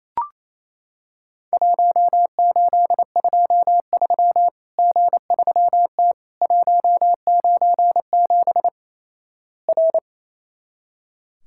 Morse code, heard as a keyed tone, about 28 wpm, 700 Hz. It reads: 1823 G3T 197 R
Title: Worked All Europe DX Contest - QTC Exchange - 28wpm